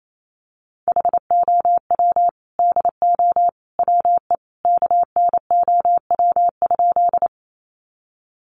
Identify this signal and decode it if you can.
Morse code, heard as a keyed tone, about 28 wpm, 700 Hz. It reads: HOW DO WE KNOW?
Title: A Sound Of Thunder 28wpm